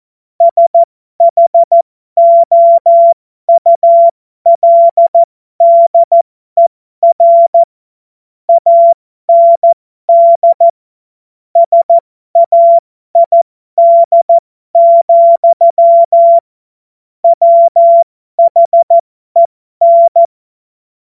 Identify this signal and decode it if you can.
Morse code, heard as a keyed tone, about 14 wpm, spaced as for 12 wpm, 680 Hz. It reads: SHOULDER AND SAID, WHEN